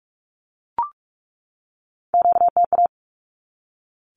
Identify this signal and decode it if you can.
Morse code, heard as a keyed tone, about 44 wpm, 700 Hz. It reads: QTA
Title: Q-Codes 45wpm